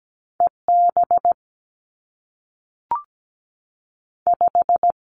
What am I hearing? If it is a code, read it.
Morse code, heard as a keyed tone, about 17 wpm, 700 Hz. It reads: E B 5
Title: Two Challenging Letter-Number Combinations 17wpm